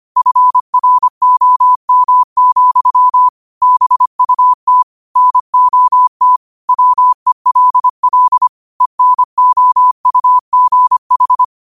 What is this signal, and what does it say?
Morse code, heard as a keyed tone, about 25 wpm, 1000 Hz. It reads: FROM, BUT NOT WELL ENOUGH
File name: qotdfiles / 25wpm